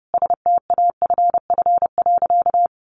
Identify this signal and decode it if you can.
Morse code, heard as a keyed tone, about 30 wpm, 700 Hz. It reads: STAFF.
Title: Sentences from Top 1500 Words 30wpm